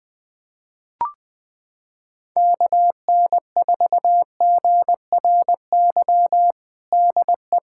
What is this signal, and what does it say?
Morse code, heard as a keyed tone, about 20 wpm, 700 Hz. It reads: KN4GRY DE
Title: USA Callsign DE Callsign 20wpm